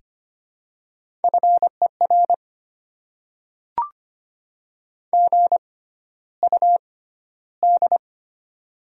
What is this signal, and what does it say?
Morse code, heard as a keyed tone, about 25 wpm, spaced as for 13 wpm, 700 Hz. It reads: FER GUD
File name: Top 20 QSO Elements - 6x Character Spacing - 25wpm